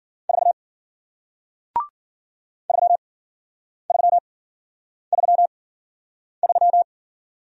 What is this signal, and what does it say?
Morse code, heard as a keyed tone, about 49 wpm, spaced as for 9 wpm, 700 Hz. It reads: 4 3333
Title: Speed-Racing Single Letter-Number 40wpm